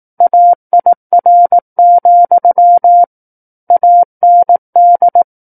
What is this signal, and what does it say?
Morse code, heard as a keyed tone, about 18 wpm, 700 Hz. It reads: AIR, AND
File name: 18 WPM